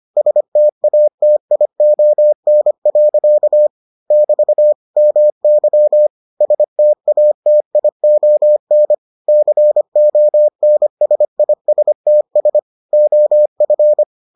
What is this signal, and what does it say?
Morse code, heard as a keyed tone, about 25 wpm, 600 Hz. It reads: STATION. <BT> MY STATION CONSISTS OF